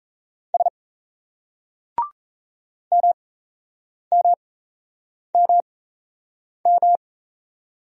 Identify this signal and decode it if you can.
Morse code, heard as a keyed tone, about 38 wpm, spaced as for 8 wpm, 700 Hz. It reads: S MMMM